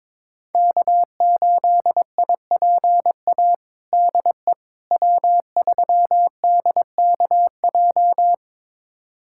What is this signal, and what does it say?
Morse code, heard as a keyed tone, about 22 wpm, 700 Hz. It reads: K8IPA DE W3DKJ
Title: USA Callsign DE Callsign - 5 Characters - 22wpm